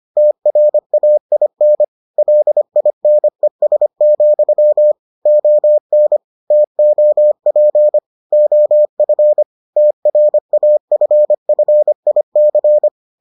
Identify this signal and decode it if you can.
Morse code, heard as a keyed tone, about 25 wpm, 600 Hz. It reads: TRAIN LINES, ON TOP OF TRAFFIC